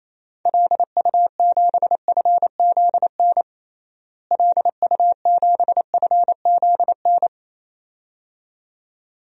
Morse code, heard as a keyed tone, about 27 wpm, 700 Hz. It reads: LU7FZN LU7FZN
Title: DXCC-100 Call Signs - 6 Characters - Repeated 2x - 28wpm